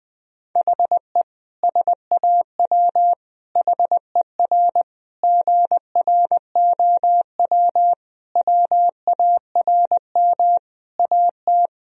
Morse code, heard as a keyed tone, about 20 wpm, 700 Hz. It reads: HE SAW HER GROW WARM AT
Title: Sentences from Top 1000 Words - Review Part 2 - 20wpm